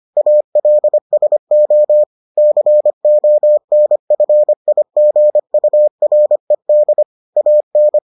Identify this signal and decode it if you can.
Morse code, heard as a keyed tone, about 25 wpm, 600 Hz. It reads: ALSO CONFIGURED AN